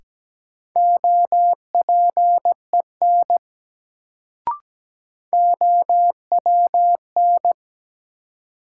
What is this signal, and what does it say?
Morse code, heard as a keyed tone, about 17 wpm, 700 Hz. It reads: OPEN OWN
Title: Sentences from Top 300 Words - New Word Drill - 17wpm